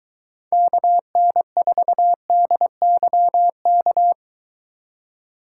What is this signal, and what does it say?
Morse code, heard as a keyed tone, about 22 wpm, 700 Hz. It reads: KN4DYK